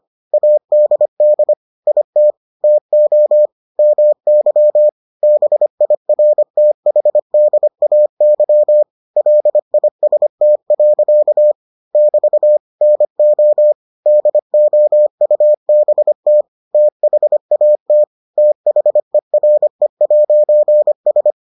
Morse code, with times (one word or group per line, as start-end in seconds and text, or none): 0.3-1.5 ADD
1.9-2.3 IT
2.6-3.5 TO
3.8-4.9 MY
5.2-8.8 BIRTHDAY
9.2-11.5 LIST.
11.9-12.6 <BT>
12.8-13.7 NO
14.1-16.4 DOUBT
16.7-18.0 THAT
18.4-21.3 THERE'S